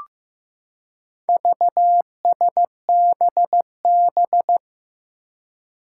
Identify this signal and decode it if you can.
Morse code, heard as a keyed tone, about 16 wpm, 700 Hz. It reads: VSBB
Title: Four Challenging Letter-Number Combinations 15wpm